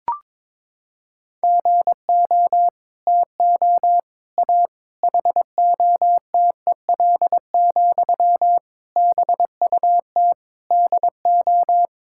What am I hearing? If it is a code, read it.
Morse code, heard as a keyed tone, about 22 wpm, 700 Hz. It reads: GO TO A HOTEL, BUT DO
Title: Sentences from Top 1600 Words 22wpm